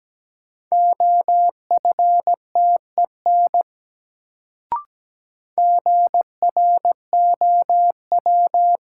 Morse code, heard as a keyed tone, about 17 wpm, 700 Hz. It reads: OFTEN GROW
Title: Sentences from Top 300 Words - New Word Drill - 17wpm